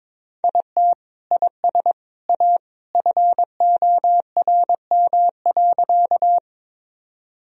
Morse code, heard as a keyed tone, about 22 wpm, 700 Hz. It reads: IT IS A FORM.
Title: Sentences from Top 1000 Words - Review Part 1 - 22wpm